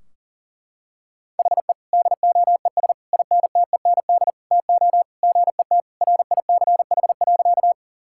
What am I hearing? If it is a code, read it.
Morse code, heard as a keyed tone, about 40 wpm, 700 Hz. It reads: HE DOES INTEND TO GET RICH.
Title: Sentences from Top 1600 Words 40wpm